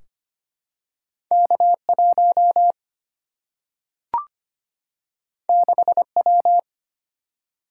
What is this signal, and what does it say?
Morse code, heard as a keyed tone, about 25 wpm, 700 Hz. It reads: K 1 6 W
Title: Single-Letter Single-Number 25wpm